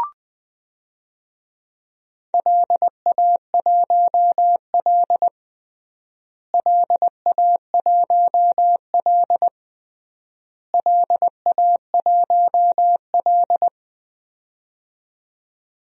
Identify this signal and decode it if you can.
Morse code, heard as a keyed tone, about 20 wpm, 700 Hz. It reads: LA1L LA1L LA1L
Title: DXCC-100 Call Signs - 4 Characters - Repeated 3x - 20wpm